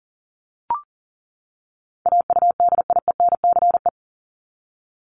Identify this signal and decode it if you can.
Morse code, heard as a keyed tone, about 40 wpm, 700 Hz. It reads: AUDIENCE
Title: Top 751-1000 Words 40wpm